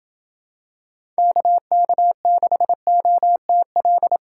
Morse code, heard as a keyed tone, about 27 wpm, 700 Hz. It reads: KK6OTL